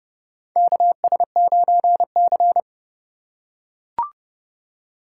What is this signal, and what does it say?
Morse code, heard as a keyed tone, about 30 wpm, 700 Hz. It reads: KS9C
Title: USA Call Signs - 2x Character Spacing - 30wpm